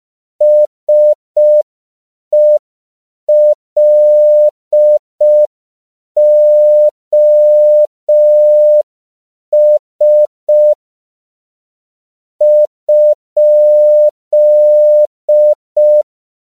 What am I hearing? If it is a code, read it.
Morse code, heard as a keyed tone, about 5 wpm, 600 Hz. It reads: SELOS?
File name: CW / 5PPM